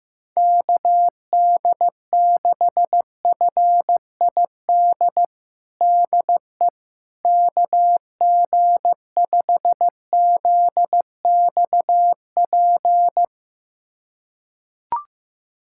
Morse code, heard as a keyed tone, about 15 wpm, 700 Hz. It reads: KD6FID DE KG5ZXP
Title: USA Callsign DE Callsign - 6 Characters - 15wpm